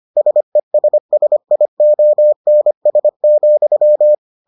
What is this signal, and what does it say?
Morse code, heard as a keyed tone, about 25 wpm, 600 Hz. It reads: SESSIONS,